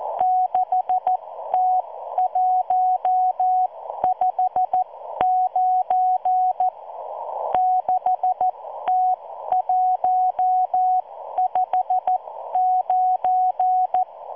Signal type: Morse code